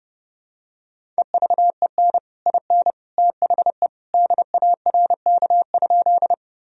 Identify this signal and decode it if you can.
Morse code, heard as a keyed tone, about 30 wpm, 700 Hz. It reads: EVEN IN THE DARK?
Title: Sentences from Top 1000 Words - Review Part 1 - 30wpm